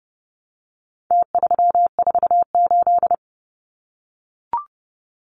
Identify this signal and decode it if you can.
Morse code, heard as a keyed tone, about 30 wpm, 700 Hz. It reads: T348
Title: Numbers - Contest Style - 4-Digit 30wpm